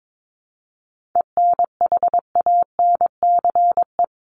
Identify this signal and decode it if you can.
Morse code, heard as a keyed tone, about 22 wpm, 700 Hz. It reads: ENHANCE
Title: Top 1251-1500 Words 22wpm